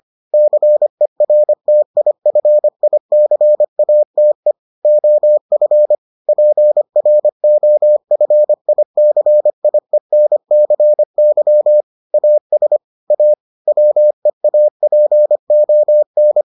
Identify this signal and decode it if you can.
Morse code, heard as a keyed tone, about 25 wpm, 600 Hz. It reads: CERTIFICATE OF PROFICIENCY AS A WEAPON